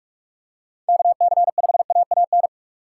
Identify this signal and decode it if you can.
Morse code, heard as a keyed tone, about 44 wpm, 700 Hz. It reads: KC5AAN